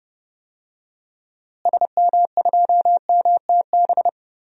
Morse code, heard as a keyed tone, about 30 wpm, 700 Hz. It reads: SM2MTB
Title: DXCC-100 Call Signs - 6 Characters - Repeated 2x - 30wpm